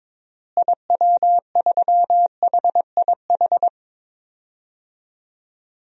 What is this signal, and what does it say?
Morse code, heard as a keyed tone, about 22 wpm, 700 Hz. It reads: IW3HIH
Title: DXCC-100 Call Signs - 6 Characters - Repeated 2x - 22wpm